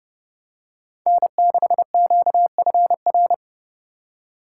Morse code, heard as a keyed tone, about 31 wpm, 700 Hz. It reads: N6QFR